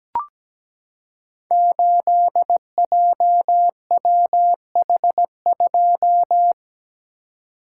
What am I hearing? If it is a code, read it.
Morse code, heard as a keyed tone, about 17 wpm, 700 Hz. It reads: 8JWH2
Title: Code Groups - 5 Letter:Numbers - 17wpm